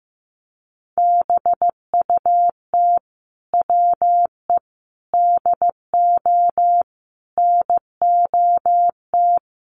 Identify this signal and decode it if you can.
Morse code, heard as a keyed tone, about 15 wpm, 700 Hz. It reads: BUT WE DO NOT
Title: Sets of 5 Words Encore2 15wpm